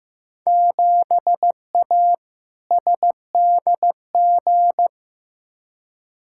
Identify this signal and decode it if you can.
Morse code, heard as a keyed tone, about 15 wpm, 700 Hz. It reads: 7A SDG
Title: Field Day Exchanges Encore - 15wpm